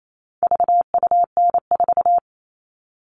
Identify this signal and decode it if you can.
Morse code, heard as a keyed tone, about 28 wpm, 700 Hz. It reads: VUN4
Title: All Cut Numbers - 4-Digit 28wpm